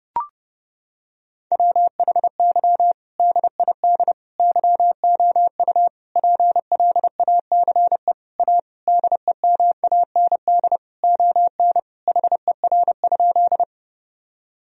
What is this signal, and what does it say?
Morse code, heard as a keyed tone, about 31 wpm, 700 Hz. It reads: WHY DID YOU PLACE A DEMAND ON HER?
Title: Sentences from Top 1000 Words - Review Part 4 - 30wpm